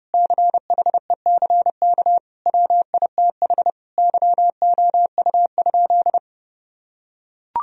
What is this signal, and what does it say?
Morse code, heard as a keyed tone, about 28 wpm, 700 Hz. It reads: CHECK WITH YOU?
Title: Sentences from Top 1000 Words - Review Part 3 - 30wpm